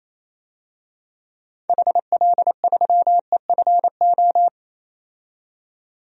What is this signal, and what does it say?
Morse code, heard as a keyed tone, about 27 wpm, 700 Hz. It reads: HL3EFO
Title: DXCC-100 Call Signs - 6 Characters - 28wpm